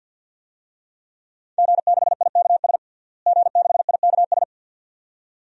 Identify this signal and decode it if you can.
Morse code, heard as a keyed tone, about 53 wpm, 700 Hz. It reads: G6IKS G6IKS
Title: DXCC-100 Call Signs - 5 Characters - Repeated 2x - 50wpm